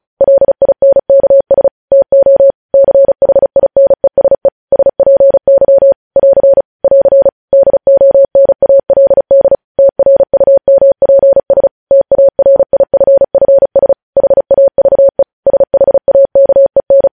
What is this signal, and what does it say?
Morse code, heard as a keyed tone, about 36 wpm, 550 Hz. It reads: LINKS TO CHINESE SPY <AR> <AR> DONALD TRUMPS TARIFFS HAVE SHAKEN
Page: news headlines in morse code at 35 wpm